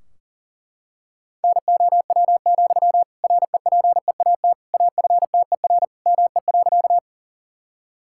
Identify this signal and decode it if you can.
Morse code, heard as a keyed tone, about 38 wpm, 700 Hz. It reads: NOW, REPEAT AFTER ME.
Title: Sentences from Top 1600 Words 40wpm